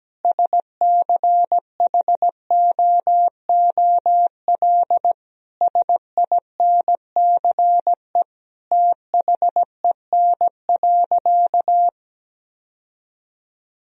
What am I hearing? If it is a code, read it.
Morse code, heard as a keyed tone, about 17 wpm, 700 Hz. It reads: SCHOOL SINCE THEN.
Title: Sentences from Top 1000 Words - Review Part 1 - 17wpm